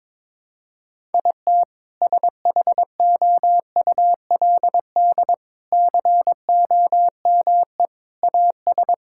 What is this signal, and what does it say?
Morse code, heard as a keyed tone, about 21 wpm, 700 Hz. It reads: IT SHOULD COME AS
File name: practice / 22wpm / Sets of 8 Words - 22wpm